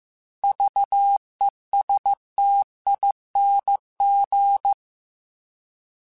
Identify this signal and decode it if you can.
Morse code, heard as a keyed tone, about 15 wpm, 800 Hz. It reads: VESTING
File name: Alphabetical / Alphabetical mp3s / Vs @13